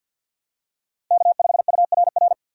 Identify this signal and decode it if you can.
Morse code, heard as a keyed tone, about 49 wpm, 700 Hz. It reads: K5URR
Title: USA Call Signs - 5 Characters - 3x Character Spacing - 50wpm